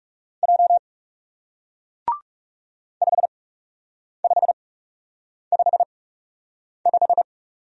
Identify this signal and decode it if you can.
Morse code, heard as a keyed tone, about 40 wpm, spaced as for 8 wpm, 700 Hz. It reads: J 5555